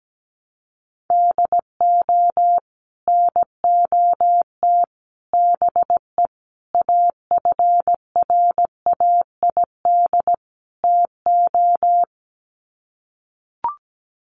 Morse code, heard as a keyed tone, about 17 wpm, 700 Hz. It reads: DO NOT BE AFRAID TO
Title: Sets of 5 Words Encore2 17wpm